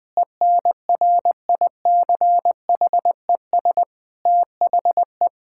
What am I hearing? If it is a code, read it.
Morse code, heard as a keyed tone, about 20 wpm, 700 Hz. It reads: ENRICHES THE
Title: Difficult Sentences with Hints - 20wpm